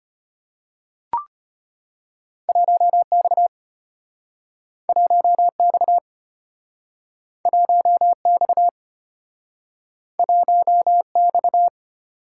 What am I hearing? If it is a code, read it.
Morse code, heard as a keyed tone, about 33 wpm, 700 Hz. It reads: 1X 1X 1X 1X